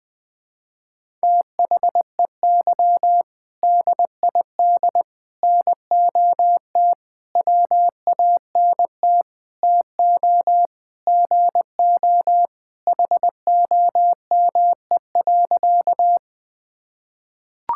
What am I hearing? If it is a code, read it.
Morse code, heard as a keyed tone, about 20 wpm, 700 Hz. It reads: THEY DID NOT WANT TO GO HOME.
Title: Sentences with 5-Word Phrases - Spoken Beginning - 20wpm